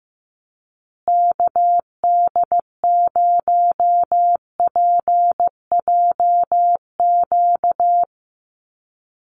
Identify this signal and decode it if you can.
Morse code, heard as a keyed tone, about 15 wpm, 700 Hz. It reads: KD0PJQ
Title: USA Call Signs Repeated 2x 15wpm